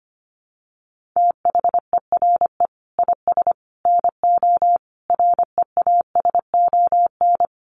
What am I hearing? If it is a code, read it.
Morse code, heard as a keyed tone, about 25 wpm, 700 Hz. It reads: THERE IS NO REASON
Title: Sets of 5 Words Encore 25wpm